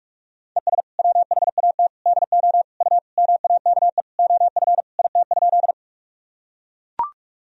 Morse code, heard as a keyed tone, about 44 wpm, 700 Hz. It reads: ES WHAT DO U MAKE OF IT?
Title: Sentences from Top 100 Words and Abbreviations 45wpm